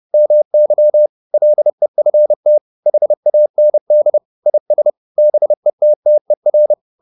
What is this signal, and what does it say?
Morse code, heard as a keyed tone, about 30 wpm, 600 Hz. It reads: MY LEFT HAND IS BETTER